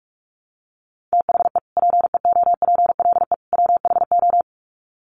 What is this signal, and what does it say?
Morse code, heard as a keyed tone, about 44 wpm, 700 Hz. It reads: THE PEOPLE WHO
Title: Sets of 3 Words 45wpm